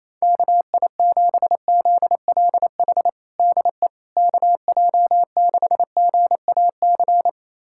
Morse code, heard as a keyed tone, about 27 wpm, 700 Hz. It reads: KI7ZLH DE KJ6GAC
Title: USA Callsign DE Callsign - 6 Characters - 28wpm